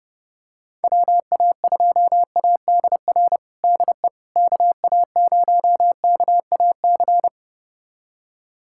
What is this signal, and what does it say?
Morse code, heard as a keyed tone, about 30 wpm, 700 Hz. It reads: WA2ADR DE KA0KAC
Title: USA Callsign DE Callsign - 6 Characters - 30wpm